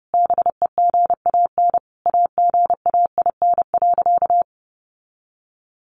Morse code, heard as a keyed tone, about 30 wpm, 700 Hz. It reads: BEGAN AGAIN.
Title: Sentences from Top 1300 Words 30wpm